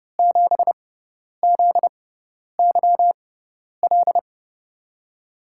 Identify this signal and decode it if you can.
Morse code, heard as a keyed tone, about 30 wpm, spaced as for 10 wpm, 700 Hz. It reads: 7ZYL
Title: USA Call Signs - 6x Character Spacing - 30wpm